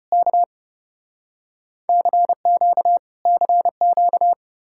Morse code, heard as a keyed tone, about 34 wpm, 700 Hz. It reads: K CQ CQ